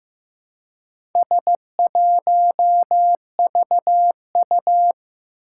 Morse code, heard as a keyed tone, about 15 wpm, 700 Hz. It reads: S1VU
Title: Four Challenging Letter-Number Combinations 15wpm